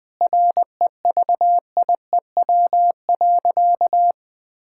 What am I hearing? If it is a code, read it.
Morse code, heard as a keyed tone, about 20 wpm, 700 Hz. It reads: REVIEW.
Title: Sentences from Top 1000 Words - Review Part 6 - 20wpm